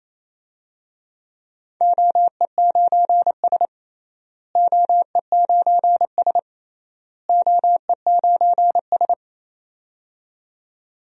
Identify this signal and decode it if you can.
Morse code, heard as a keyed tone, about 28 wpm, 700 Hz. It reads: OE9S OE9S OE9S
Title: DXCC-100 Call Signs - 4 Characters - Repeated 3x - 28wpm